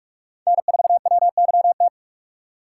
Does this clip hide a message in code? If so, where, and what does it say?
Yes: Morse code from 0.5 to 1.9 s, heard as a keyed tone, about 44 wpm, 700 Hz. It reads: N4WYT